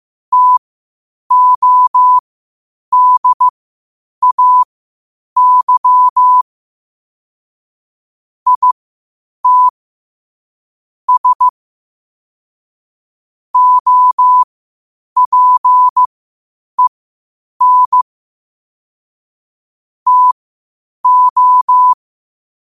Morse code, heard as a keyed tone, about 15 wpm, spaced as for 8 wpm, 1000 Hz. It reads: TODAY IT S OPEN TO
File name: qotdfiles / 8wpm